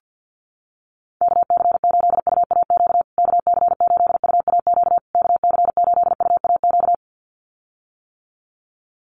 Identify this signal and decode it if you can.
Morse code, heard as a keyed tone, about 48 wpm, 700 Hz. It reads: KC8UAQ KC8UAQ KC8UAQ
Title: USA Call Signs Repeated 3x 50wpm